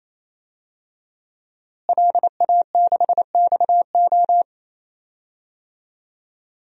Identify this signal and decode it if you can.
Morse code, heard as a keyed tone, about 28 wpm, 700 Hz. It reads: LA6XO
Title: DXCC-100 Call Signs - 5 Characters - Repeated 3x - 28wpm